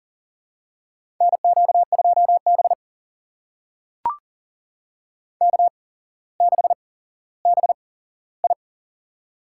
Morse code, heard as a keyed tone, about 40 wpm, spaced as for 17 wpm, 700 Hz. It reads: NQ2B K6BI
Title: USA Call Signs - 4 Characters - 8x Character Spacing - 40wpm